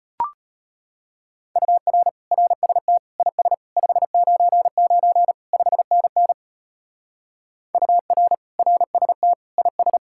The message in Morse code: UR RST IS 599 5NN UR RST IS

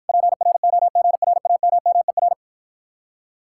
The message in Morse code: PROGRAMMER

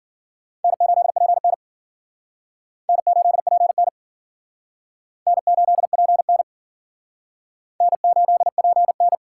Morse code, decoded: N8PN N8PN N8PN N8PN

